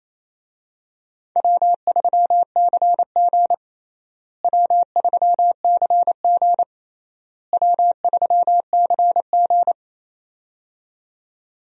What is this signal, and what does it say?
Morse code, heard as a keyed tone, about 28 wpm, 700 Hz. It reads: W3CG W3CG W3CG
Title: USA Call Signs - 4 Characters - Repeated 3x - 28wpm